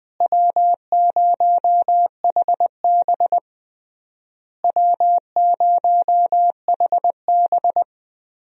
Morse code, heard as a keyed tone, about 20 wpm, 700 Hz. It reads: W0HB W0HB
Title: USA Call Signs - 4 Characters - Repeated 3x - 20wpm